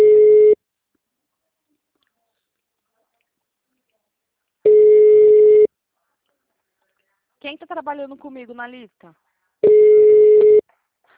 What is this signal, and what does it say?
Ringback tone (Europe, 425 Hz), with speech.
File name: out- -2006- - - .224